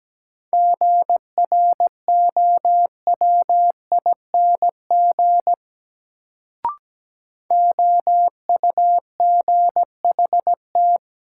Morse code, heard as a keyed tone, about 17 wpm, 700 Hz. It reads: GROWING OUGHT
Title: Sets of 5 Words Encore - New Word Drill - 17wpm